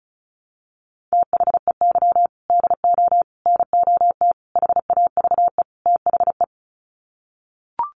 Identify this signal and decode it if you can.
Morse code, heard as a keyed tone, about 34 wpm, 700 Hz. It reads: THEY DO NOT HAVE THE
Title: Sets of 5 Words 35wpm